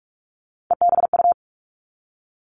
Morse code, heard as a keyed tone, about 46 wpm, 700 Hz. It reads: EBU
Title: Three Challenging Letter-Number Combinations 45wpm